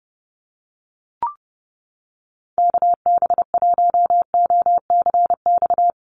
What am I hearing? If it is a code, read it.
Morse code, heard as a keyed tone, about 30 wpm, 700 Hz. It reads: KB1OCX
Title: USA Call Signs Repeated 3x 30wpm